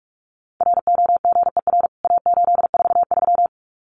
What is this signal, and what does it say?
Morse code, heard as a keyed tone, about 44 wpm, 700 Hz. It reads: ROGER A843
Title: CWT Exchange 45wpm